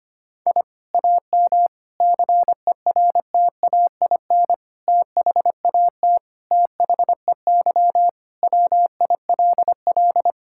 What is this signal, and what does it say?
Morse code, heard as a keyed tone, about 25 wpm, 700 Hz. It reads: I AM CERTAIN THAT THEY WILL
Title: Sentences from Top 1000 Words - Review Part 6 - 25wpm